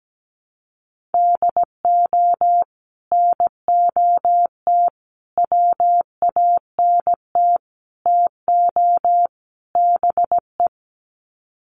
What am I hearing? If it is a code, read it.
Morse code, heard as a keyed tone, about 17 wpm, 700 Hz. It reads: DO NOT WANT TO BE
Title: Sets of 5 Words 17wpm